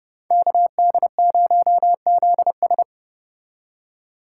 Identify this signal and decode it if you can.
Morse code, heard as a keyed tone, about 31 wpm, 700 Hz. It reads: KD0ZS